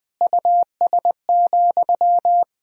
Morse code, heard as a keyed tone, about 20 wpm, 700 Hz. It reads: US,
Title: Sentences from Top 1000 Words - Review Part 1 - 20wpm